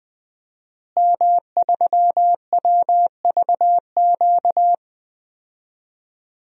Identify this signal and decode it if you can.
Morse code, heard as a keyed tone, about 21 wpm, 700 Hz. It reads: M3WVQ
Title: DXCC-100 Call Signs - 5 Characters - Repeated 2x - 20wpm